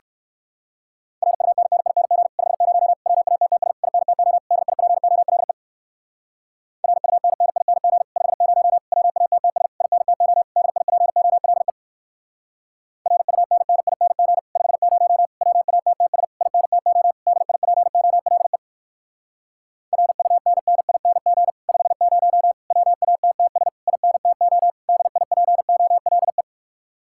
Morse code, heard as a keyed tone, about 58 wpm, 700 Hz. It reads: RUNNING 50 WATTS INTO DIPOLE RUNNING 50 WATTS INTO DIPOLE RUNNING 50 WATTS INTO DIPOLE RUNNING 50 WATTS INTO DIPOLE